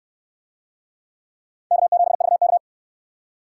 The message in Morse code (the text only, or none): K7VK